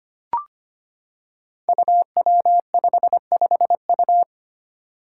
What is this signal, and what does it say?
Morse code, heard as a keyed tone, about 25 wpm, 700 Hz. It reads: UW55U
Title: Five Challenging Letter-Number Combinations 25wpm